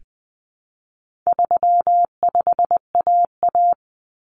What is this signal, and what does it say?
Morse code, heard as a keyed tone, about 20 wpm, 700 Hz. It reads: 35AA
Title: Common Cut Numbers - 4-Digit 20wpm